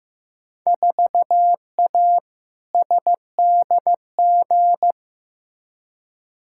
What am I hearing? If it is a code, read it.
Morse code, heard as a keyed tone, about 15 wpm, 700 Hz. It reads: 4A SDG
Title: Field Day Exchanges - 15wpm